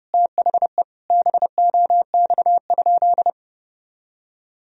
Morse code, heard as a keyed tone, about 30 wpm, 700 Hz. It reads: THE BOX?
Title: Sentences from Top 1000 Words - Review Part 2 - 30wpm